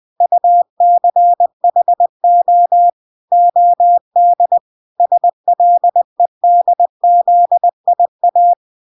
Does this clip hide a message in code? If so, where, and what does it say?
Yes: Morse code from 0.2 to 8.5 s, heard as a keyed tone, about 20 wpm, 700 Hz. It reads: UCHO OD SLEDZIA